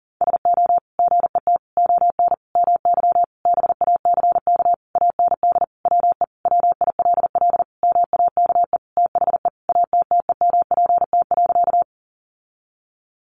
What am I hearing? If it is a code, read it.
Morse code, heard as a keyed tone, about 40 wpm, 700 Hz. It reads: SO GET ON MY BACK AND WE WILL MAKE THE ATTEMPT.
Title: Sentences from Top 1000 Words 40wpm